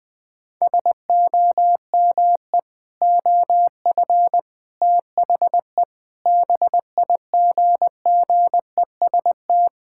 Morse code, heard as a keyed tone, about 21 wpm, 700 Hz. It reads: SOME OF THE BIGGEST
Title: Sentences with 7-Word Phrases - 20wpm